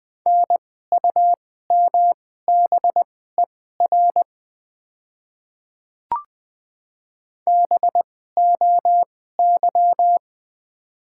Morse code, heard as a keyed tone, about 20 wpm, spaced as for 15 wpm, 700 Hz. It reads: NUMBER BOY
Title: 2x Character Spacing - Top 200 Words - 20wpm